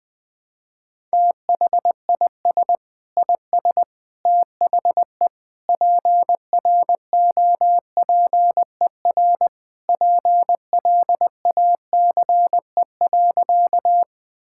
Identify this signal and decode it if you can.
Morse code, heard as a keyed tone, about 21 wpm, 700 Hz. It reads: THIS IS THE PROPER PLACE.
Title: Sentences from Top 1000 Words - Review Part 5 - 20wpm